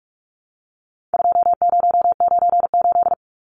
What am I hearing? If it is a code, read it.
Morse code, heard as a keyed tone, about 44 wpm, 700 Hz. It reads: 2098
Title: Numbers - 4-Digit 45wpm